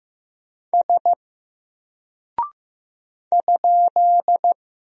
Morse code, heard as a keyed tone, about 15 wpm, 700 Hz. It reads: S ?
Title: Single Character (Letter:Number:Punctuation:Prosign) Alt 15wpm